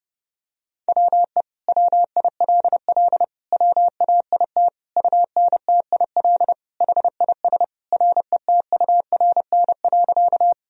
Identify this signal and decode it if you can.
Morse code, heard as a keyed tone, about 30 wpm, 700 Hz. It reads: WE WILL WAIT UNTIL HIS RETURN.
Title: Sentences from Top 1000 Words - Review Part 1 - 30wpm